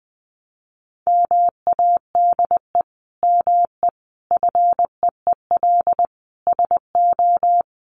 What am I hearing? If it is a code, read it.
Morse code, heard as a keyed tone, about 20 wpm, 700 Hz. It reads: MADE ME FEEL SO
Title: Sentences from Top 400 Words 20wpm